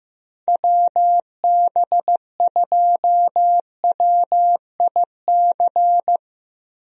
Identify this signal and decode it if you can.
Morse code, heard as a keyed tone, about 16 wpm, 700 Hz. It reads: WB2WIC
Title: USA Call Signs - 6 Characters - Repeated 3x - 15wpm